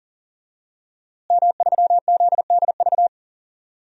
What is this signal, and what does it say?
Morse code, heard as a keyed tone, about 40 wpm, 700 Hz. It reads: M3ZDV
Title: DXCC-100 Call Signs - 5 Characters - Repeated 3x - 40wpm